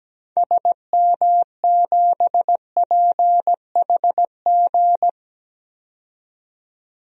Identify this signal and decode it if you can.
Morse code, heard as a keyed tone, about 17 wpm, 700 Hz. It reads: SM7PHG
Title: DXCC-100 Call Signs - 6 Characters - 17wpm